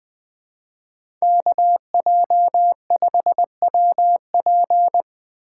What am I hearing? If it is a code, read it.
Morse code, heard as a keyed tone, about 20 wpm, 700 Hz. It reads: KJ5WP
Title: USA Call Signs - 5 Characters - 4x Character Spacing - 20wpm